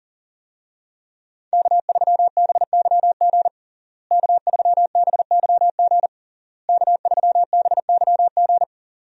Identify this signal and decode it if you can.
Morse code, heard as a keyed tone, about 40 wpm, 700 Hz. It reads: K3BYG K3BYG K3BYG
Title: USA Call Signs - 5 Characters - Repeated 3x - 40wpm